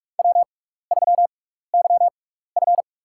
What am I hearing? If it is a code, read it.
Morse code, heard as a keyed tone, about 46 wpm, spaced as for 15 wpm, 700 Hz. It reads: W3YF
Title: USA Call Signs - 4 Characters - 6x Character Spacing - 45wpm